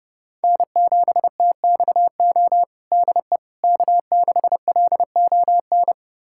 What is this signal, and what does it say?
Morse code, heard as a keyed tone, about 31 wpm, 700 Hz. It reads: N7TXO DE K6LON
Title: USA Callsign DE Callsign - 5 Characters - 30wpm